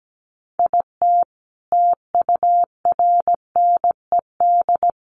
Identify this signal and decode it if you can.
Morse code, heard as a keyed tone, about 17 wpm, 700 Hz. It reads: IT TURNED
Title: Sets of 5 Words 17wpm